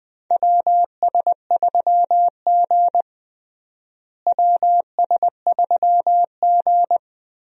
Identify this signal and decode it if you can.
Morse code, heard as a keyed tone, about 21 wpm, 700 Hz. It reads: WS3G WS3G
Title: USA Call Signs - 4 Characters - Repeated 2x - 20wpm